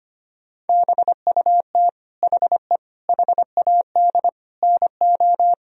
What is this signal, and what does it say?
Morse code, heard as a keyed tone, about 25 wpm, 700 Hz. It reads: BUT HE HAD NO
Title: Sentences from Top 1000 Words - Review Part 4 - 25wpm